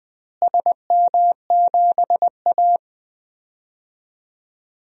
Morse code, heard as a keyed tone, about 20 wpm, 700 Hz. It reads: SM7A
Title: DXCC-100 Call Signs - 4 Characters - Repeated 3x - 20wpm